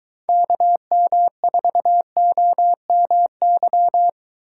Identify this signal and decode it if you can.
Morse code, heard as a keyed tone, about 22 wpm, 700 Hz. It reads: KM4OMY